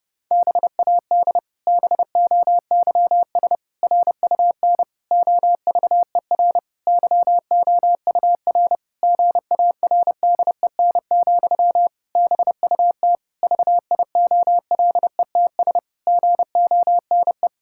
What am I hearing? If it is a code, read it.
Morse code, heard as a keyed tone, about 30 wpm, 700 Hz. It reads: BAD BOYS RUN OVER YOUR GARDEN, BUT VIOLETS GONE
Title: Pangram Challenge v2 - 30wpm